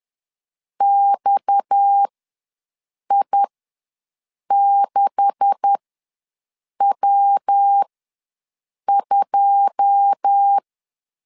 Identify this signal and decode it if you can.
Morse code, heard as a keyed tone, about 10 wpm, spaced as for 6 wpm, 800 Hz. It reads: XI6W2
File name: LEZIONI / 9 286 ws 10wpm 4032